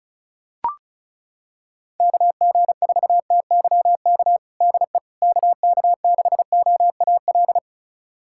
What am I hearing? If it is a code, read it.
Morse code, heard as a keyed tone, about 34 wpm, 690 Hz. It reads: KG4TYK DE KK6OAL
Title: USA Callsign DE Callsign - 6 Characters - 35wpm